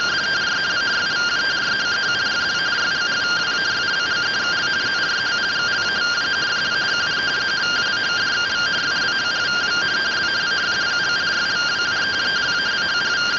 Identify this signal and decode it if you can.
Two-tone teleprinter signal, about 170 Hz shift.